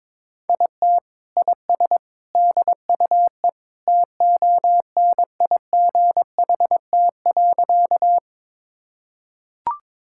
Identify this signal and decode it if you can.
Morse code, heard as a keyed tone, about 22 wpm, 700 Hz. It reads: IT IS DUE TONIGHT.
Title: Sentences from Top 1000 Words - Review Part 6 - 22wpm